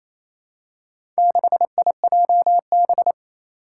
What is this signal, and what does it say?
Morse code, heard as a keyed tone, about 27 wpm, 700 Hz. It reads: 6IJB
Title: Four Challenging Letter-Number Combinations 28wpm